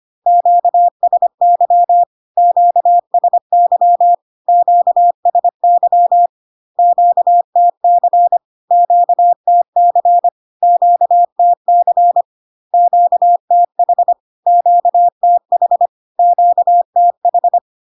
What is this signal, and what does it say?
Morse code, heard as a keyed tone, about 25 wpm, 700 Hz. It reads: QSY QSY QSY QTC QTC QTC QTH QTH QTH
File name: Q-codes most common repeated 3 times 25 25wpm